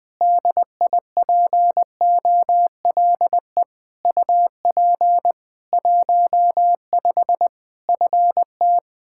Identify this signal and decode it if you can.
Morse code, heard as a keyed tone, about 20 wpm, 700 Hz. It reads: DIPOLE UP 15 FT
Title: K1USN SST QSO Practice 20wpm